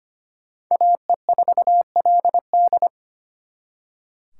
Morse code, heard as a keyed tone, about 31 wpm, 700 Hz. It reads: AE4LD
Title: Speed Racing - USA Call Signs 25wpm